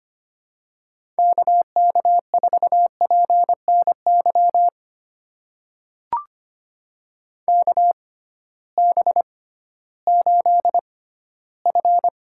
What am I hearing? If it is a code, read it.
Morse code, heard as a keyed tone, about 25 wpm, spaced as for 15 wpm, 700 Hz. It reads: KK4PNY KB8F
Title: USA Call Signs - 6 Characters - 6x Character Spacing - 25wpm